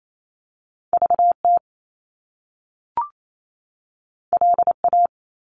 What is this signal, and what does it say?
Morse code, heard as a keyed tone, about 28 wpm, 700 Hz. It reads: VT LA
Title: US States 28wpm